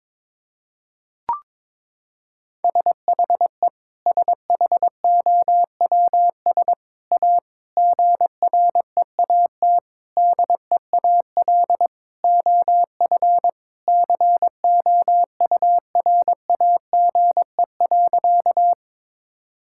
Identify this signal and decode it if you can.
Morse code, heard as a keyed tone, about 21 wpm, 700 Hz. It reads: SHE SHOWS A GREAT DEAL OF COURAGE.
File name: practice / 22wpm / Sentences with 4-Word Phrases - 22wpm